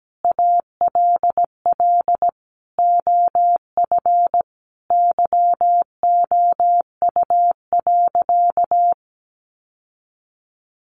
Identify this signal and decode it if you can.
Morse code, heard as a keyed tone, about 17 wpm, 700 Hz. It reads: ALL OF YOU.
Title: Sentences from Top 1400 Words 17wpm